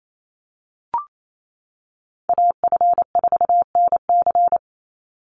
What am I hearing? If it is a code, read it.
Morse code, heard as a keyed tone, about 28 wpm, 700 Hz. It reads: AF4NC
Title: Call Signs - 28wpm